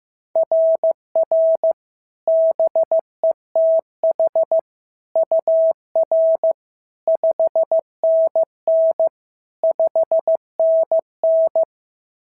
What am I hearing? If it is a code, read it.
Morse code, heard as a keyed tone, about 14 wpm, 650 Hz. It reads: RR BETH UR 5NN 5NN
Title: SOTA QSO Practice 15wpm